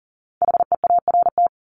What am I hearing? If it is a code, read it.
Morse code, heard as a keyed tone, about 40 wpm, 700 Hz. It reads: HEART
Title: Top 1001-1250 Words 40wpm